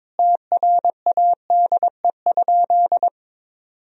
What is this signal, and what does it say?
Morse code, heard as a keyed tone, about 22 wpm, 700 Hz. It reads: TRADE?
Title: Sentences from Top 1000 Words - Review Part 4 - 22wpm